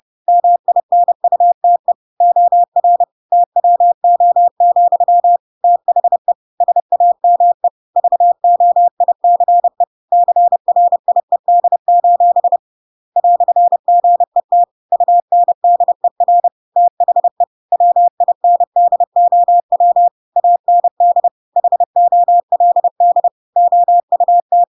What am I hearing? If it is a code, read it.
Morse code, heard as a keyed tone, about 30 wpm, 700 Hz. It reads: MINUTE OR TWO, THE SAME VOICE CRIED: "GET UNDER THE WINDOW AND HOLD OUT
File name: Pinocchio / Pinocchio ch6 30 30 ik1hge